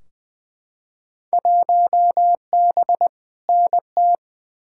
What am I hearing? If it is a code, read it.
Morse code, heard as a keyed tone, about 20 wpm, 700 Hz. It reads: 1B NT
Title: Field Day Exchanges Encore - 20wpm